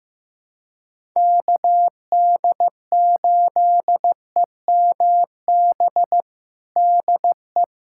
Morse code, heard as a keyed tone, about 15 wpm, 700 Hz. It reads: KD8EMB DE
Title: USA Callsign DE Callsign 15wpm